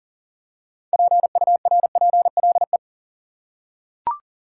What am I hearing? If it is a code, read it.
Morse code, heard as a keyed tone, about 40 wpm, 700 Hz. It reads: PURPLE
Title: Six Letter Words - 40wpm